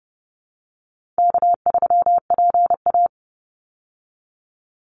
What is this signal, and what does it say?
Morse code, heard as a keyed tone, about 30 wpm, 700 Hz. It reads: K3PA
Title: Easy Call Signs - 30wpm